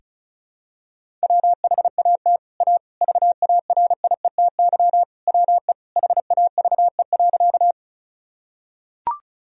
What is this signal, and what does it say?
Morse code, heard as a keyed tone, about 34 wpm, 700 Hz. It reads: WHAT A VARIETY WE HAVE.
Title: Sentences from Top 1000 Words - Review Part 4 - 35wpm